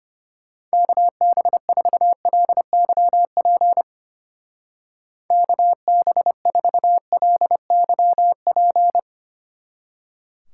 Morse code, heard as a keyed tone, about 32 wpm, 700 Hz. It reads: KB4LYP KB4LYP